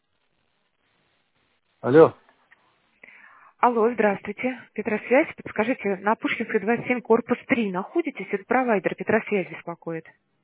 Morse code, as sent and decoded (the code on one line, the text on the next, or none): none